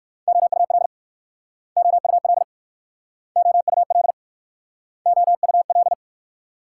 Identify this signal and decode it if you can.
Morse code, heard as a keyed tone, about 60 wpm, 700 Hz. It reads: OUL OUL OUL OUL